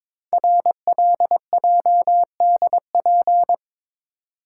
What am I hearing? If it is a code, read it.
Morse code, heard as a keyed tone, about 22 wpm, 700 Hz. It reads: RLJDP
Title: Code Groups - 5 Letters - 22wpm